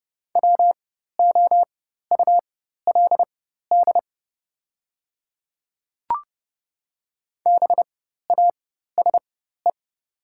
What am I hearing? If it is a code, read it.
Morse code, heard as a keyed tone, about 30 wpm, spaced as for 14 wpm, 700 Hz. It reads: WOULD BASE
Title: 4x Character Spacing - Top 500 Words - 30wpm